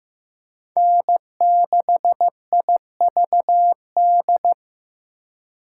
Morse code, heard as a keyed tone, about 18 wpm, 700 Hz. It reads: N6IVD
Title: Speed Racing - USA 1x3 Call Signs 15wpm